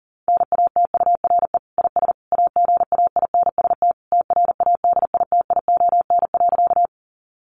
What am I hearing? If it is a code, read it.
Morse code, heard as a keyed tone, about 40 wpm, 700 Hz. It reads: NATURE IS AGAINST TRADITION.
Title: Sentences from Top 1500 Words 40wpm